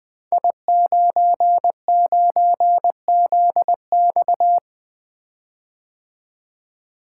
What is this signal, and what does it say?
Morse code, heard as a keyed tone, about 20 wpm, 700 Hz. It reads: I99ZX
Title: Code Groups - 5 Letter:Numbers - 20wpm